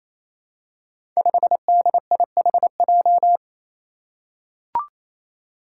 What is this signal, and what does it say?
Morse code, heard as a keyed tone, about 28 wpm, 700 Hz. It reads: 5DIHJ
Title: Five Challenging Letter-Number Combinations 28wpm